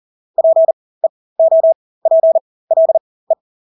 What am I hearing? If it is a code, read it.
Morse code, heard as a keyed tone, about 40 wpm, spaced as for 20 wpm, 650 Hz. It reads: PEOPLE
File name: Engelske ord 40-20 wpm